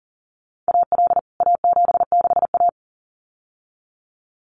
Morse code, heard as a keyed tone, about 40 wpm, 700 Hz. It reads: AL A76A
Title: CWT Exchange 40wpm